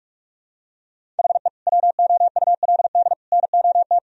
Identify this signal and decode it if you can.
Morse code, heard as a keyed tone, about 46 wpm, 700 Hz. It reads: HE WOULD NOT
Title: Sentences from Top 1000 Words - Review Part 1 - 45wpm